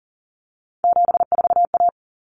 Morse code, heard as a keyed tone, about 40 wpm, 700 Hz. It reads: 74A
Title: Common Cut Numbers - 3-Digit 40wpm